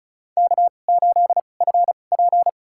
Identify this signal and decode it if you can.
Morse code, heard as a keyed tone, about 34 wpm, spaced as for 25 wpm, 700 Hz. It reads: K8FP
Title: USA Call Signs - 4 Characters - 2x Character Spacing - 35wpm